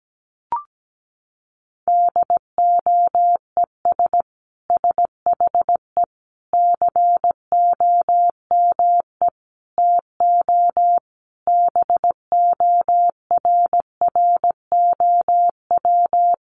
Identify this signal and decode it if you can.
Morse code, heard as a keyed tone, about 17 wpm, 700 Hz. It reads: DOES SHE COME TO BORROW
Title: Sentences from Top 1500 Words 17wpm